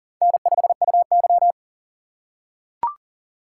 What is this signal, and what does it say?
Morse code, heard as a keyed tone, about 40 wpm, 700 Hz. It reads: N5UY
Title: USA Call Signs - 4 Characters - 3x Character Spacing - 40wpm